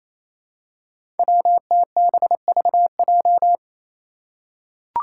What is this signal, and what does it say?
Morse code, heard as a keyed tone, about 28 wpm, 700 Hz. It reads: WTBVJ
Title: Five Challenging Letter-Number Combinations 28wpm